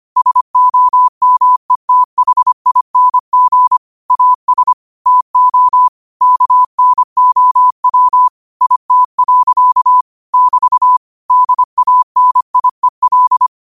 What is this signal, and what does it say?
Morse code, heard as a keyed tone, about 25 wpm, 1000 Hz. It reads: SOMETHING AS TO KNOW IT. <BT> DANIEL
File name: qotdfiles / 25wpm